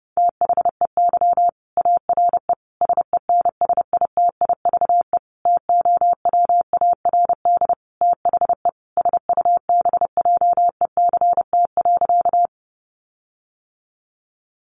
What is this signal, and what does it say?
Morse code, heard as a keyed tone, about 28 wpm, 700 Hz. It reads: THEY ARE SENSITIVE TOWARD THE SUBJECT.
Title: Sentences from Top 1400 Words 30wpm